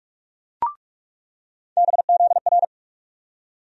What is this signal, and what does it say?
Morse code, heard as a keyed tone, about 46 wpm, 700 Hz. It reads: BZR
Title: USA Call Sign Postfixes - 3 Letters 45wpm